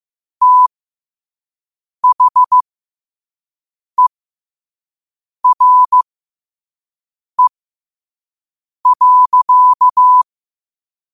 Morse code, heard as a keyed tone, about 15 wpm, spaced as for 5 wpm, 1000 Hz. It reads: THERE.
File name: qotdfiles / 5wpm